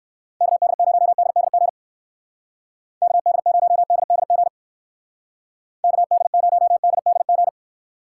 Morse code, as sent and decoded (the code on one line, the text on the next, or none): -.- -.. ----- -.. -.. --. / -.- -.. ----- -.. -.. --. / -.- -.. ----- -.. -.. --.
KD0DDG KD0DDG KD0DDG